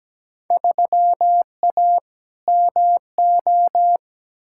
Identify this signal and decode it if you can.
Morse code, heard as a keyed tone, about 17 wpm, 700 Hz. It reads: 3A MO
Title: Field Day Exchanges Encore - 17wpm